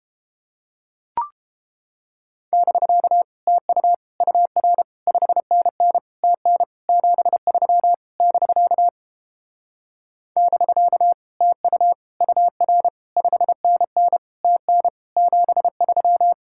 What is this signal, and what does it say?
Morse code, heard as a keyed tone, about 31 wpm, 700 Hz. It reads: <BK> TU UR 5NN TN 73 <BK> <BK> TU UR 5NN TN 73